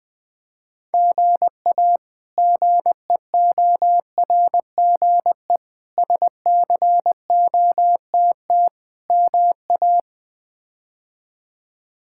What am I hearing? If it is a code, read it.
Morse code, heard as a keyed tone, about 20 wpm, 700 Hz. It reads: GA GEORGE SCOTT MA
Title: K1USN SST QSO Practice - 20wpm